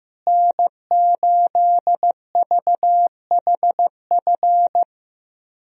Morse code, heard as a keyed tone, about 15 wpm, 700 Hz. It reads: N8VHF
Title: USA Call Signs - 5 Characters - Repeated 2x - 15wpm